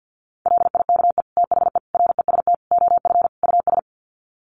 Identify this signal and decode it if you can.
Morse code, heard as a keyed tone, about 50 wpm, 700 Hz. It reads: LIKE THE REST OF US
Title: Sets of 5 Words Encore2 50wpm